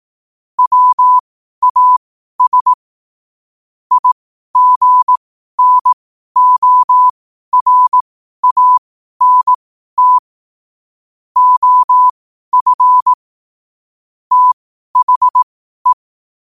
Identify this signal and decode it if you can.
Morse code, heard as a keyed tone, about 18 wpm, spaced as for 12 wpm, 1000 Hz. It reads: WAS IGNORANT OF THE
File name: qotdfiles / 12wpm